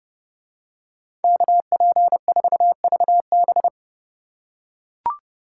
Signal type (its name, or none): Morse code